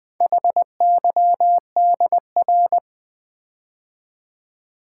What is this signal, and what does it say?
Morse code, heard as a keyed tone, about 20 wpm, 700 Hz. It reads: HYDR
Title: Build A Word - 4 Letters At A Time - 8-Letter Words - 20wpm